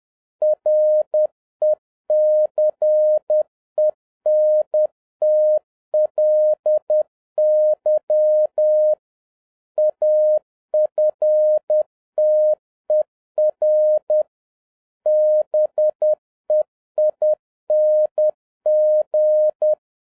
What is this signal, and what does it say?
Morse code, heard as a keyed tone, about 10 wpm, 610 Hz. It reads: RECENTLY AFTER BEING